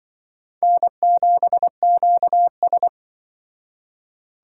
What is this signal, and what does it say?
Morse code, heard as a keyed tone, about 24 wpm, 700 Hz. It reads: N7QS
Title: Speed Racing - USA 1x2 Call Signs 20wpm